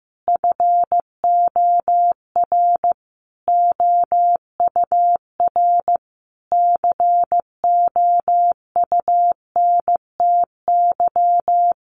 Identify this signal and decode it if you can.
Morse code, heard as a keyed tone, about 14 wpm, 700 Hz. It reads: FOR OUR COUNTY
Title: Sentences from Top 1500 Words 15wpm